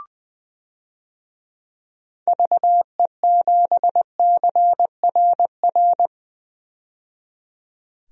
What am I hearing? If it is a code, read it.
Morse code, heard as a keyed tone, about 20 wpm, 700 Hz. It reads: VE7CRR
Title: DXCC-100 Call Signs - 6 Characters - 20wpm